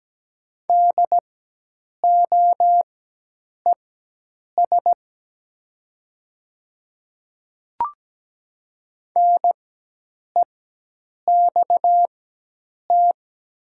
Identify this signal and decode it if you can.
Morse code, heard as a keyed tone, about 17 wpm, spaced as for 8 wpm, 700 Hz. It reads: DOES NEXT
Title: 4x Character Spacing - Top 300 Words - 17wpm